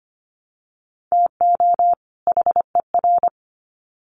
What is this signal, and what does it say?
Morse code, heard as a keyed tone, about 25 wpm, 700 Hz. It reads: TO HER
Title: Sets of 2 Words 25wpm